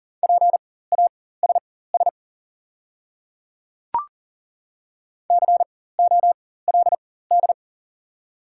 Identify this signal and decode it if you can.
Morse code, heard as a keyed tone, about 40 wpm, spaced as for 19 wpm, 690 Hz. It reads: PASS COLD
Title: 4x Character Spacing - Top 500 Words - 40wpm